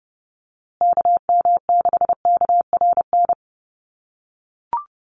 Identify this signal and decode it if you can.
Morse code, heard as a keyed tone, about 30 wpm, 700 Hz. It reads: KM6KRN
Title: USA Call Signs Repeated 2x 30wpm